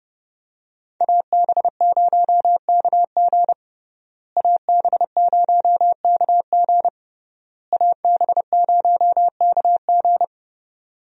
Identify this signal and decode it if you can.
Morse code, heard as a keyed tone, about 32 wpm, 700 Hz. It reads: AB0KG AB0KG AB0KG
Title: USA Call Signs - 5 Characters - Repeated 3x - 30wpm